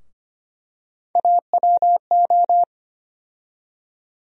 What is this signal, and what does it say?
Morse code, heard as a keyed tone, about 25 wpm, 700 Hz. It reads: AWO
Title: USA Call Sign Postfixes - 3 Letters 25wpm